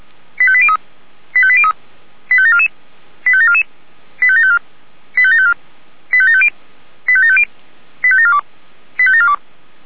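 Five-tone selective call (ZVEI), heard as quick runs of five tones, tones of about 70 ms. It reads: 85803 85803 85744 85744 85754 85754 85750 85750 85742 85742
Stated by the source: ZVEI five-tone call received by SDR.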